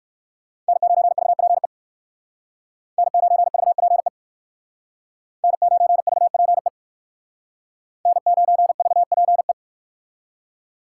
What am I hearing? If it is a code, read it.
Morse code, heard as a keyed tone, about 57 wpm, 700 Hz. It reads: N9VPE N9VPE N9VPE N9VPE